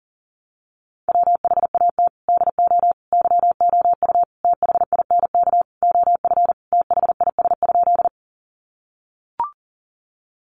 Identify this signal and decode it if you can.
Morse code, heard as a keyed tone, about 40 wpm, 700 Hz. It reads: WHAT DO YOU THINK OF THIS?
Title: Sentences from Top 200 Words 40wpm